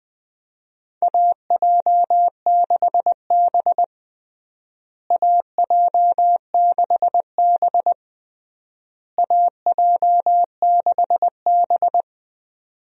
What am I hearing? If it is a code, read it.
Morse code, heard as a keyed tone, about 21 wpm, 700 Hz. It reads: AJ6B AJ6B AJ6B
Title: USA Call Signs - 4 Characters - Repeated 3x - 20wpm